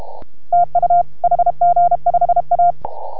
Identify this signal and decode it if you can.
Morse code, heard as a keyed tone, about 33 wpm, 690 Hz. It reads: TU HG5A